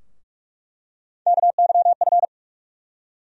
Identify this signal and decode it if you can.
Morse code, heard as a keyed tone, about 46 wpm, 700 Hz. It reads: KYF
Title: USA Call Sign Postfixes - 3 Letters 45wpm